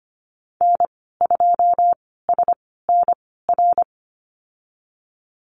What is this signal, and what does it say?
Morse code, heard as a keyed tone, about 25 wpm, spaced as for 16 wpm, 690 Hz. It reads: N2SNR
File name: Lesson 15g - Number 5 - Call Signs 25wpm